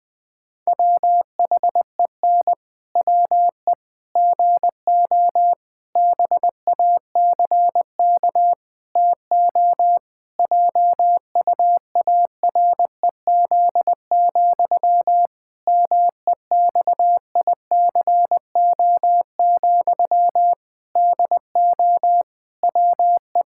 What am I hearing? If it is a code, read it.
Morse code, heard as a keyed tone, about 21 wpm, 700 Hz. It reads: WHEN WE GO BACK TO JUAREZ, MEXICO, DO WE
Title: Pangram Challenge v2 - 20wpm